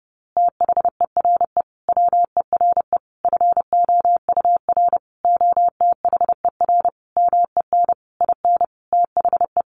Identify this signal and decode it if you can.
Morse code, heard as a keyed tone, about 30 wpm, 700 Hz. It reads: THERE WERE FOUR OTHER MEN IN THE
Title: A Sound Of Thunder 30wpm